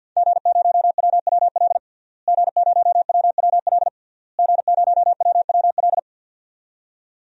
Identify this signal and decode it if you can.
Morse code, heard as a keyed tone, about 52 wpm, 700 Hz. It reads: G0WWL G0WWL G0WWL
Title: DXCC-100 Call Signs - 5 Characters - Repeated 3x - 50wpm